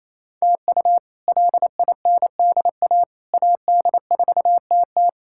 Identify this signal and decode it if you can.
Morse code, heard as a keyed tone, about 28 wpm, 700 Hz. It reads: TU LINDA AD4TT
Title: K1USN SST QSO Practice - 28wpm